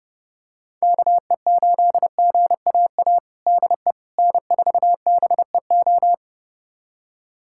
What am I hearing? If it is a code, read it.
Morse code, heard as a keyed tone, about 30 wpm, 700 Hz. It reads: KE8GAA DE N4BEO
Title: USA Callsign DE Callsign 30wpm